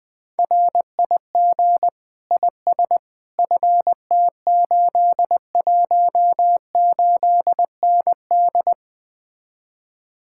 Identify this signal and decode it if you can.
Morse code, heard as a keyed tone, about 20 wpm, 700 Hz. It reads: RIG IS FT818ND